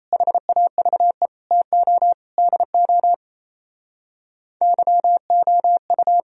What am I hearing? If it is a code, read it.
Morse code, heard as a keyed tone, about 35 wpm, 700 Hz. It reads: HAVE TO DO YOU
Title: Speed-Racing - Sets of 4 Words 28wpm